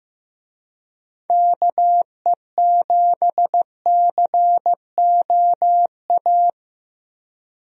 Morse code, heard as a keyed tone, about 15 wpm, 700 Hz. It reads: KE7COA
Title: USA Call Signs - 6 Characters - 15wpm